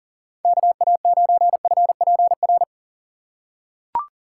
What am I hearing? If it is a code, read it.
Morse code, heard as a keyed tone, about 39 wpm, 700 Hz. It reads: KA9FPR
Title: USA Call Signs - 6 Characters - 4x Character Spacing - 40wpm